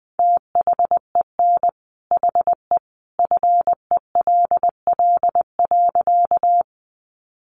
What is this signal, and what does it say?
Morse code, heard as a keyed tone, about 20 wpm, 700 Hz. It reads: THEN HE FELL.
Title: A Sound Of Thunder 20wpm